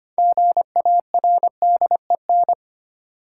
Morse code, heard as a keyed tone, about 25 wpm, 700 Hz. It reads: GARDEN
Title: Sentences from Top 700 Words - New Word Drill - 25wpm